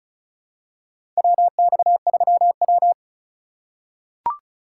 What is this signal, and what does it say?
Morse code, heard as a keyed tone, about 34 wpm, 700 Hz. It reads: WX3W
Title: USA Call Signs - 4 Characters - 2x Character Spacing - 35wpm